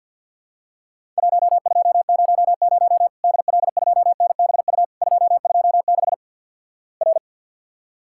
Morse code, heard as a keyed tone, about 48 wpm, 700 Hz. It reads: 1200 DL2NBU 226 R
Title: Worked All Europe DX Contest - QTC Exchange - 50wpm